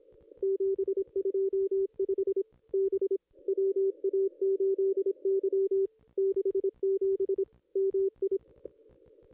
Morse code, heard as a keyed tone, about 26 wpm, 380 Hz. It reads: 725 B WA8Y 67 MI
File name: contests / ss-cw-k5tr-op-n5rz -2200